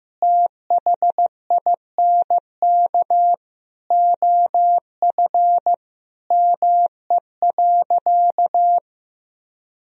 Morse code, heard as a keyed tone, about 16 wpm, 700 Hz. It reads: THINK OF ME.
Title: Sentences from Top 1000 Words - Review Part 1 - 15wpm